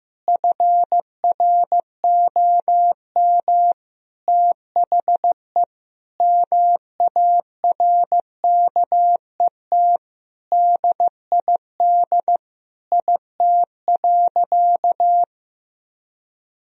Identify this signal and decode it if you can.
Morse code, heard as a keyed tone, about 14 wpm, 700 Hz. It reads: FROM THE MARKET DID IT.
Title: Sentences from Top 1000 Words - Review Part 3 - 15wpm